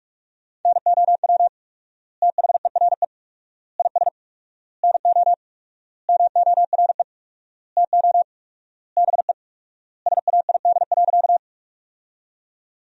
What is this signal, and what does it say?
Morse code, heard as a keyed tone, about 46 wpm, 700 Hz. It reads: NOW THERE IS NO MORE TO BE SAID.
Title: 4x Word-Spacing - Sentences from Top 100 Words - 45wpm